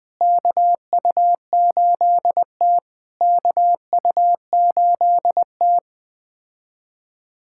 Morse code, heard as a keyed tone, about 20 wpm, 700 Hz. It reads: KU8T KU8T
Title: POTA QSO Practice - 20wpm